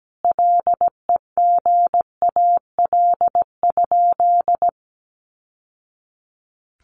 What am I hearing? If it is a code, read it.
Morse code, heard as a keyed tone, about 17 wpm, 700 Hz. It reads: LEGAL?
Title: Sentences from Top 1200 Words 17wpm